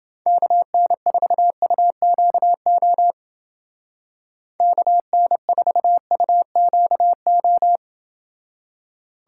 Morse code, heard as a keyed tone, about 29 wpm, 700 Hz. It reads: KN4UQO KN4UQO